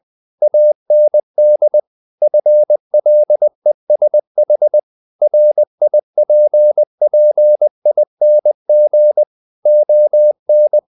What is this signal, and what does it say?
Morse code, heard as a keyed tone, about 20 wpm, 600 Hz. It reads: AND FLESH RIPPING ON
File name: the old man and the sea part 2 20wpm